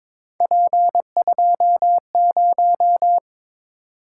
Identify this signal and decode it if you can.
Morse code, heard as a keyed tone, about 22 wpm, 700 Hz. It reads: P20
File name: practice / 22wpm / Three Letter-Number Combinations - 22wpm